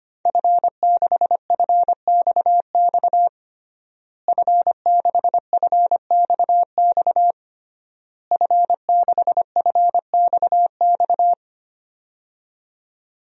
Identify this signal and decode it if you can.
Morse code, heard as a keyed tone, about 25 wpm, 700 Hz. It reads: F6FXX F6FXX F6FXX
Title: DXCC-100 Call Signs - 5 Characters - Repeated 3x - 25wpm